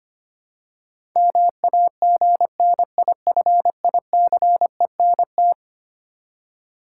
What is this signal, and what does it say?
Morse code, heard as a keyed tone, about 25 wpm, 700 Hz. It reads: MAGNIFICENT
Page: morse-code-ninja-practice-25wpm